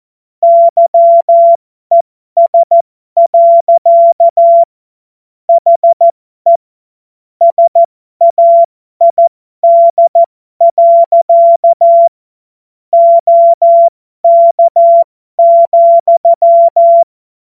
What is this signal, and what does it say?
Morse code, heard as a keyed tone, about 14 wpm, spaced as for 12 wpm, 680 Hz. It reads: YES. HE SAID. OK,
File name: Morsekurs / Witze